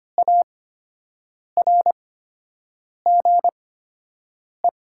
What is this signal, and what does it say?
Morse code, heard as a keyed tone, about 25 wpm, spaced as for 7 wpm, 700 Hz. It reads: ARGE
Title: 8x Character Spacing - Top 500 Words - 25wpm